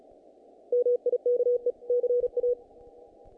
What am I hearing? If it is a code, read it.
Morse code, heard as a keyed tone, about 37 wpm, 490 Hz. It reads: MIKE CA
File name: contests / naqp-cw-k5tr-op-k5pi- -1800